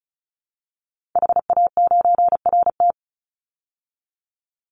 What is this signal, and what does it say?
Morse code, heard as a keyed tone, about 34 wpm, 700 Hz. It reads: HA9RT
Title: Call Signs Repeated 3x - 35wpm